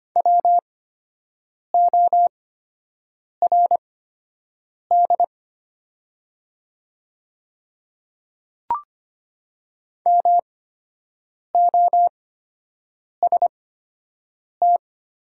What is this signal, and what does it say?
Morse code, heard as a keyed tone, about 25 wpm, spaced as for 7 wpm, 700 Hz. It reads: WORD MOST
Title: 8x Character Spacing - Top 100 Words - 25wpm